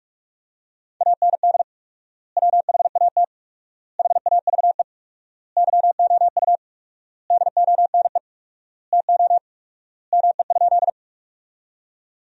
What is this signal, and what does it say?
Morse code, heard as a keyed tone, about 42 wpm, 700 Hz. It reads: AND WHAT HAVE YOU DONE TO ME?
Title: 4x Word-Spacing - Sentences from Top 500 Words - 45wpm